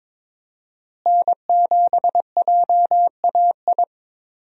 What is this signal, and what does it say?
Morse code, heard as a keyed tone, about 22 wpm, 700 Hz. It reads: N7JAI
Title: USA Call Signs - 5 Characters - Repeated 3x - 22wpm